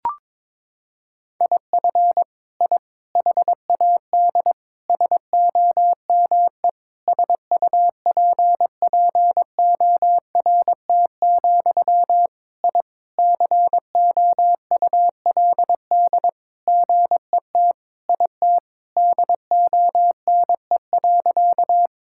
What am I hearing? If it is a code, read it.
Morse code, heard as a keyed tone, about 22 wpm, 700 Hz. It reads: IF I HAD SOME SUPPORT, I COULD GET IT DONE.
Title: Sentences from Top 1000 Words - Review Part 5 - 22wpm